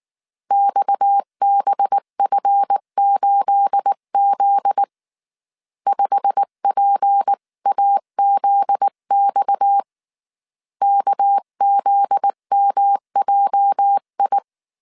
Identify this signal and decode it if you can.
Morse code, heard as a keyed tone, about 19 wpm, 800 Hz. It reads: XBF8Z 5PAZX KZMJI
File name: CW / DET / 1 9 1500C S B 18wpm 2551